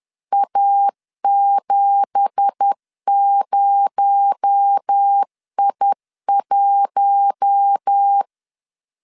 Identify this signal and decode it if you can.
Morse code, heard as a keyed tone, about 10 wpm, 800 Hz. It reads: A70I1
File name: LEZIONI / 9 2867 10wpm 2733